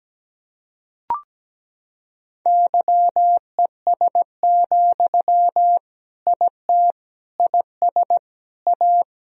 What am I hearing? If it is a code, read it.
Morse code, heard as a keyed tone, about 17 wpm, 700 Hz. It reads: YES, IT IS A
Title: Sentences from Top 1000 Words - Review Part 2 - 17wpm